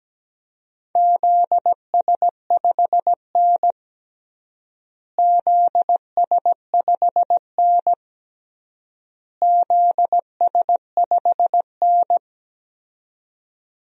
Morse code, heard as a keyed tone, about 17 wpm, 700 Hz. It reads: ZS5N ZS5N ZS5N
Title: DXCC-100 Call Signs - 4 Characters - Repeated 3x - 17wpm